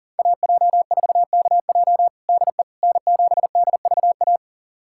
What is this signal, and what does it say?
Morse code, heard as a keyed tone, about 40 wpm, 700 Hz. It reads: AJ4KJ DE N7DVA
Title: USA Callsign DE Callsign - 5 Characters - 40wpm